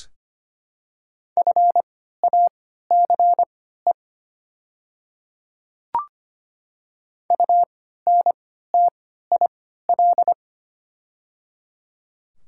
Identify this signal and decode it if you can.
Morse code, heard as a keyed tone, about 25 wpm, spaced as for 14 wpm, 700 Hz. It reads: FACE UNTIL
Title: 3x Character Spacing - Top 300 Words - 25wpm